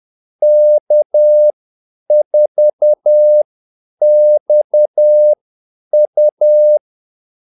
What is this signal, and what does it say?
Morse code, heard as a keyed tone, about 10 wpm, spaced as for 8 wpm, 600 Hz. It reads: K4XU
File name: Callsigns-10-8wpm